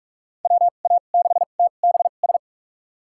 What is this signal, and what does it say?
Morse code, heard as a keyed tone, about 44 wpm, spaced as for 32 wpm, 700 Hz. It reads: WA6TBS
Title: USA Call Signs - 6 Characters - 2x Character Spacing - 45wpm